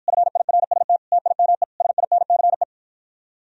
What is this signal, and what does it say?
Morse code, heard as a keyed tone, about 52 wpm, 700 Hz. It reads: FIRST TIME SINCE